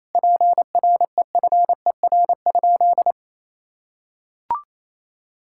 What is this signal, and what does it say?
Morse code, heard as a keyed tone, about 28 wpm, 700 Hz. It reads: PREFER?
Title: Sentences from Top 1600 Words 28wpm